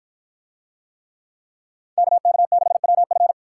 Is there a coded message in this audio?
Yes: Morse code from 2.0 to 3.3 s, heard as a keyed tone, about 49 wpm, 700 Hz. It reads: KK6WF